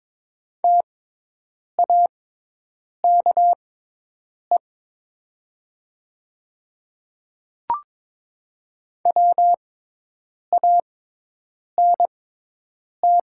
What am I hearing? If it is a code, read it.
Morse code, heard as a keyed tone, about 22 wpm, spaced as for 8 wpm, 700 Hz. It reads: TAKE WANT
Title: 6x Character Spacing - Top 200 Words - 22wpm